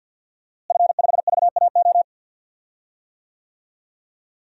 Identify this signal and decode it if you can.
Morse code, heard as a keyed tone, about 50 wpm, 700 Hz. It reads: F5VAO
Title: DXCC-100 Call Signs - 5 Characters - Repeated 3x - 50wpm